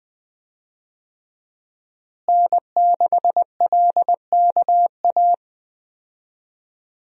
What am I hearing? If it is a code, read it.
Morse code, heard as a keyed tone, about 20 wpm, 700 Hz. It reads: N6LKA
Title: USA Call Signs - 5 Characters - Repeated 3x - 20wpm